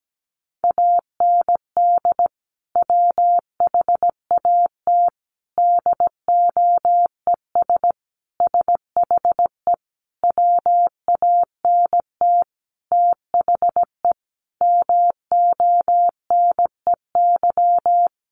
Morse code, heard as a keyed tone, about 17 wpm, 700 Hz. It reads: AND WHAT DOES SHE WANT THE MONEY
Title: Sentences from Top 400 Words 17wpm